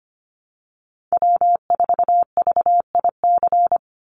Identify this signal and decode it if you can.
Morse code, heard as a keyed tone, about 25 wpm, 700 Hz. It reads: W4VIC
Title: Call Signs Repeated 2x - 25wpm